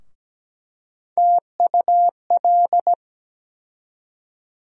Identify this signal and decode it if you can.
Morse code, heard as a keyed tone, about 17 wpm, 700 Hz. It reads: TUL
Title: USA Call Sign Postfixes - 3 Letters 17wpm